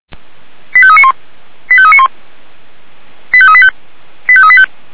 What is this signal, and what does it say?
Five-tone selective call (ZVEI), heard as quick runs of five tones, tones of about 70 ms. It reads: 85391 85391 85385 85385